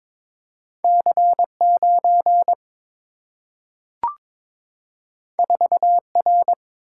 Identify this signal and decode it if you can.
Morse code, heard as a keyed tone, about 22 wpm, 700 Hz. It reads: C 9 4 R
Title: Single-Letter Single-Number 22wpm